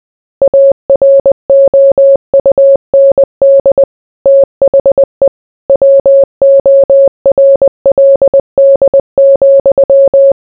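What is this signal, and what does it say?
Morse code, heard as a keyed tone, about 20 wpm, 550 Hz. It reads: AROUND THE WORLD,
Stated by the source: Morse code mixed with noise and music as background sound for sleep.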